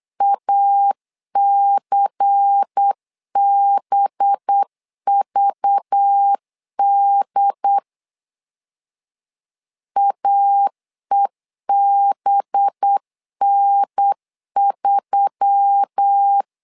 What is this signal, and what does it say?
Morse code, heard as a keyed tone, about 8 wpm, 800 Hz. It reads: ACBVD AEBN3
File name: LEZIONI / 5 LBVC / 5 LBVC 8wpm 2935